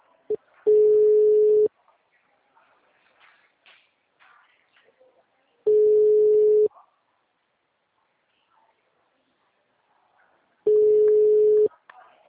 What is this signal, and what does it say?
Ringback tone (Europe, 425 Hz), with speech.